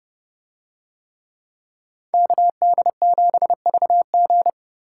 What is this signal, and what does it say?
Morse code, heard as a keyed tone, about 29 wpm, 700 Hz. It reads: KD7VG